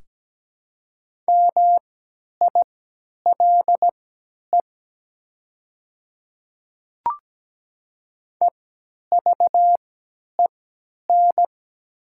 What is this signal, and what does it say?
Morse code, heard as a keyed tone, about 17 wpm, spaced as for 10 wpm, 700 Hz. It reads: MILE EVEN
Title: 3x Character Spacing - Top 300 Words - 17wpm